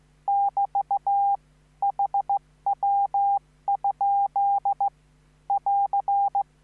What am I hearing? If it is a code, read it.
Morse code, heard as a keyed tone, about 15 wpm, 800 Hz. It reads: <BT> HW? <AR>
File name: QSO03